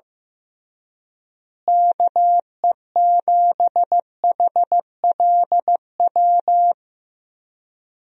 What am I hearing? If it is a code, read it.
Morse code, heard as a keyed tone, about 15 wpm, 700 Hz. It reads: KE7HLW
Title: USA Call Signs - 6 Characters - Repeated 2x - 15wpm